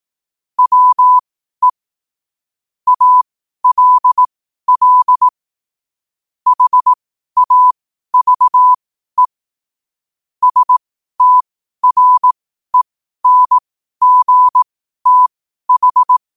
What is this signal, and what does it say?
Morse code, heard as a keyed tone, about 18 wpm, spaced as for 12 wpm, 1000 Hz. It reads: WE ALL HAVE STRENGTH
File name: qotdfiles / 12wpm